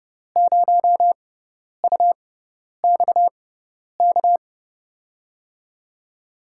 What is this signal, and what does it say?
Morse code, heard as a keyed tone, about 30 wpm, spaced as for 10 wpm, 700 Hz. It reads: 0UXK
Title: USA Call Signs - 6 Characters - 6x Character Spacing - 30wpm